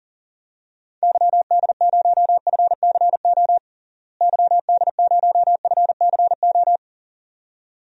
Morse code, heard as a keyed tone, about 40 wpm, 700 Hz. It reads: YD0FCO YD0FCO
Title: DXCC-100 Call Signs - 6 Characters - Repeated 2x - 40wpm